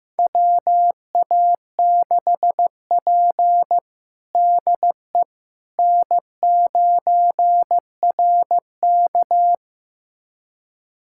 Morse code, heard as a keyed tone, about 15 wpm, 700 Hz. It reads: WA6P DE N9RK
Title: USA Callsign DE Callsign - 4 Characters - 15wpm